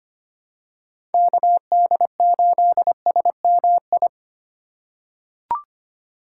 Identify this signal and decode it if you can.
Morse code, heard as a keyed tone, about 25 wpm, 700 Hz. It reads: KD8SMI
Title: USA Call Signs - 6 Characters - 2x Character Spacing - 25wpm